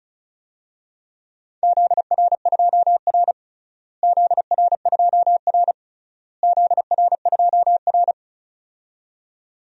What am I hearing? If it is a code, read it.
Morse code, heard as a keyed tone, about 36 wpm, 700 Hz. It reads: ZR2R ZR2R ZR2R
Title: DXCC-100 Call Signs - 4 Characters - Repeated 3x - 35wpm